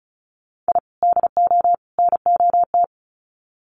Morse code, heard as a keyed tone, about 34 wpm, 700 Hz. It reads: I DO NOT
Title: Sets of 3 Words 35wpm